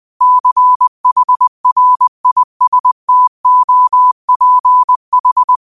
Morse code, heard as a keyed tone, about 20 wpm, 1000 Hz. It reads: CHRISTOPH
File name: qotdfiles / 20wpm